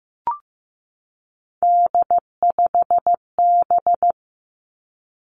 Morse code, heard as a keyed tone, about 15 wpm, 700 Hz. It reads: D5B
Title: Three Challenging Letter-Number Combinations 15wpm